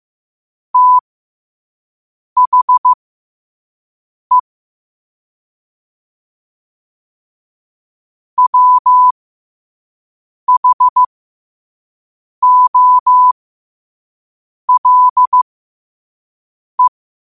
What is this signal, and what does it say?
Morse code, heard as a keyed tone, about 15 wpm, spaced as for 5 wpm, 1000 Hz. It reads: THE WHOLE
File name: 5wpm